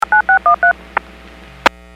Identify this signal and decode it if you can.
DTMF, heard as telephone touch tones, tones of about 110 ms. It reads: 9613